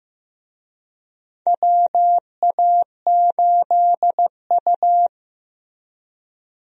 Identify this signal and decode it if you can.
Morse code, heard as a keyed tone, about 15 wpm, 700 Hz. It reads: WA8U
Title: USA Call Signs - 4 Characters - Repeated 2x - 15wpm